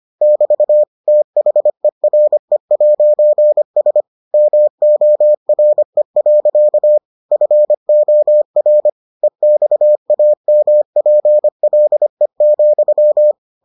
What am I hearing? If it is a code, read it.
Morse code, heard as a keyed tone, about 25 wpm, 600 Hz. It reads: <BT> THERE'S MORE. FOR EXAMPLE,